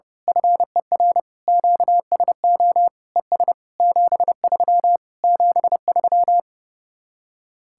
Morse code, heard as a keyed tone, about 28 wpm, 700 Hz. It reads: FER QSO ES 73 73
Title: Speed-Racing Common QSO Phrases - 30wpm